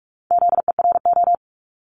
Morse code, heard as a keyed tone, about 46 wpm, 700 Hz. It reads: ZERO
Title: Multi-letter Phonemes Repeated 3x 45wpm